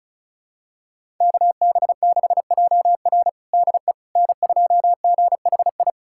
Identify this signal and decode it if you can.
Morse code, heard as a keyed tone, about 36 wpm, 700 Hz. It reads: KB6JR DE N2GHI
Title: USA Callsign DE Callsign 35wpm